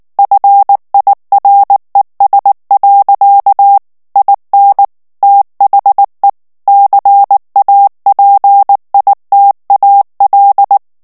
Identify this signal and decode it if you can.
Morse code, heard as a keyed tone, about 19 wpm, 800 Hz. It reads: FIRES. IN THE CAPITAL
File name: ANGLAIS texte clair 2 vitesse 18 mots minute